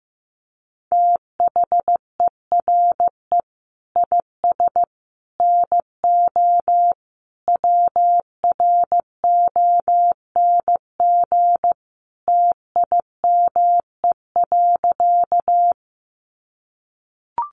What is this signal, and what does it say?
Morse code, heard as a keyed tone, about 15 wpm, 700 Hz. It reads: THERE IS NO WRONG TIME.
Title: Sentences from Top 900 Words 15wpm